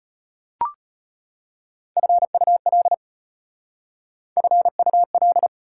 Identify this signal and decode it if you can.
Morse code, heard as a keyed tone, about 34 wpm, 700 Hz. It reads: FUL FUL